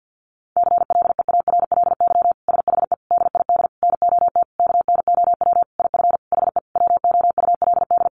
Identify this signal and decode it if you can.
Morse code, heard as a keyed tone, about 50 wpm, 700 Hz. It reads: CLEARLY SHE DID NOT KNOW IF HE WOULD
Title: Sentences from Top 1300 Words 50wpm